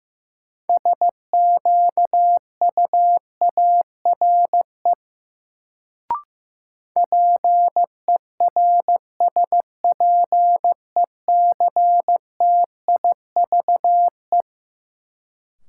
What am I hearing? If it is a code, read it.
Morse code, heard as a keyed tone, about 15 wpm, 700 Hz. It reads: SQUARE PERSPECTIVE
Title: Sentences from Top 1200 Words - New Word Drill Alt - 15wpm